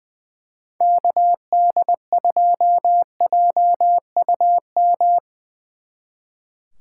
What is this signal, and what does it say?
Morse code, heard as a keyed tone, about 22 wpm, 700 Hz. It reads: KD2JUM
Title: Speed Racing - USA 2x3 Call Signs 20wpm